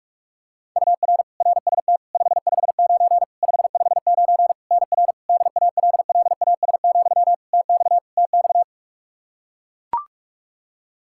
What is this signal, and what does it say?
Morse code, heard as a keyed tone, about 41 wpm, 700 Hz. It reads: UR RST 559 559 NR DALLAS, TX TX
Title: Practice QSO - Exchange 1 - Beginning - 45wpm